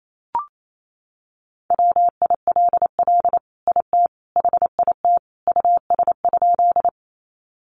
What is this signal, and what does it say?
Morse code, heard as a keyed tone, about 28 wpm, 700 Hz. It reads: WILL IT HIT US?
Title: Sentences from Top 1200 Words 28wpm